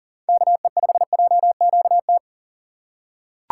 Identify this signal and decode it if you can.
Morse code, heard as a keyed tone, about 40 wpm, 700 Hz. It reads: KE5JQT
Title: USA Call Signs - 6 Characters - 6x Character Spacing - 40wpm